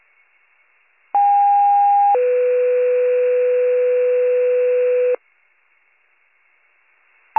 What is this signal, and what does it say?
Two-tone page, heard as a one-second tone, then a three-second tone, about 810 then 500 Hz. It is heard from 1.1 to 5.2 s.